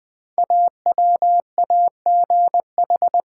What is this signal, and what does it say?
Morse code, heard as a keyed tone, about 20 wpm, 700 Hz. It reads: AWAGH
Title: Code Groups - 5 Letter:Numbers - 20wpm